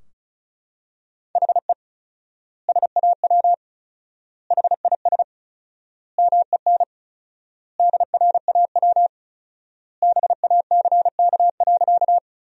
Morse code, heard as a keyed tone, about 36 wpm, 700 Hz. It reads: HE SAW HIS MEN DRAW BACK.
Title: 4x Word-Spacing - Sentences from Top 300 Words - 35wpm